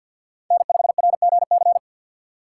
Morse code, heard as a keyed tone, about 53 wpm, 700 Hz. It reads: N5RGC
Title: USA Call Signs - 5 Characters - 2x Character Spacing - 50wpm